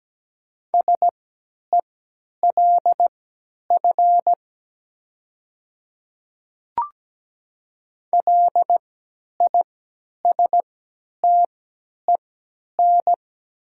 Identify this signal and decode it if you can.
Morse code, heard as a keyed tone, about 17 wpm, spaced as for 10 wpm, 700 Hz. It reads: SELF LISTEN
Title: 3x Character Spacing - Top 500 Words - 17wpm